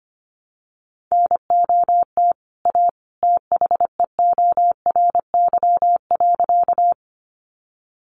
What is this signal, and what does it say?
Morse code, heard as a keyed tone, about 25 wpm, 700 Hz. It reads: NOT A THEORY.
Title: Sentences from Top 600 Words 25wpm